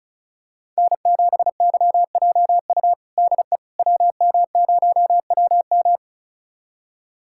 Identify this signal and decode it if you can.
Morse code, heard as a keyed tone, about 36 wpm, 690 Hz. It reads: N7YJU DE WM0WM
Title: USA Callsign DE Callsign - 5 Characters - 35wpm